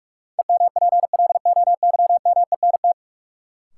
Morse code, heard as a keyed tone, about 41 wpm, 700 Hz. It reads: EMPLOYMENT
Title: Sentences from Top 1500 Words - New Word Drill - 45wpm